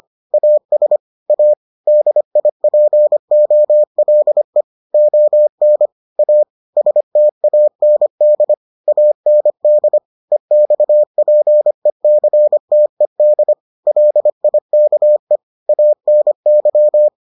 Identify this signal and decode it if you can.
Morse code, heard as a keyed tone, about 25 wpm, 600 Hz. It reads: AS A DIPOLE ON A STAND AND EXPECTED LIKE ANY